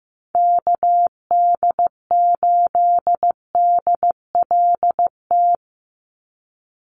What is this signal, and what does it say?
Morse code, heard as a keyed tone, about 15 wpm, 700 Hz. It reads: KD8DLT
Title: USA Call Signs 15wpm